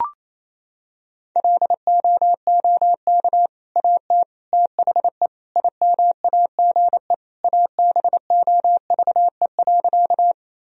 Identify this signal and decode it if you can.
Morse code, heard as a keyed tone, about 27 wpm, 700 Hz. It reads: LOOK AT THE IMAGE ABOVE.
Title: Sentences from Top 1000 Words - Review Part 4 - 28wpm